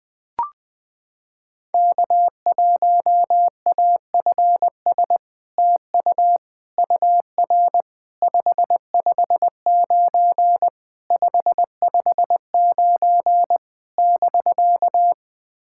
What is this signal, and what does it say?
Morse code, heard as a keyed tone, about 21 wpm, 690 Hz. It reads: K1AFS TU UR 559 559 <BK>
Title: SOTA QSO Practice 20wpm